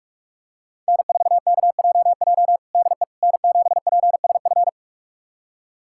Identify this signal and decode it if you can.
Morse code, heard as a keyed tone, about 46 wpm, 700 Hz. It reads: N4KJJ DE N7PSF
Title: USA Callsign DE Callsign - 5 Characters - 45wpm